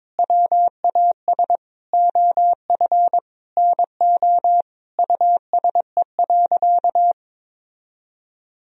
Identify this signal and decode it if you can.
Morse code, heard as a keyed tone, about 22 wpm, 700 Hz. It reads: WAS OF NO USE.
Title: Sentences from Top 1000 Words - Review Part 4 - 22wpm